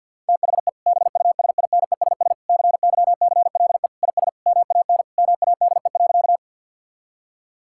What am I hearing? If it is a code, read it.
Morse code, heard as a keyed tone, about 50 wpm, 700 Hz. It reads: THE BUSINESS CYCLE IS MAN MADE.
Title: Sentences from Top 1000 Words - Review Part 6 - 50wpm